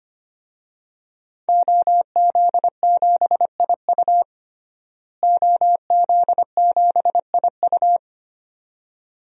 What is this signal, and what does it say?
Morse code, heard as a keyed tone, about 25 wpm, 700 Hz. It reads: OZ7IU OZ7IU
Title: DXCC-100 Call Signs - 5 Characters - Repeated 2x - 25wpm